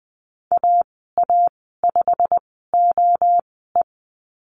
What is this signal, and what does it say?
Morse code, heard as a keyed tone, about 20 wpm, spaced as for 14 wpm, 700 Hz. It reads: AA5OE
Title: 173 Lesson 28g - Number 9 - Call Signs 20wpm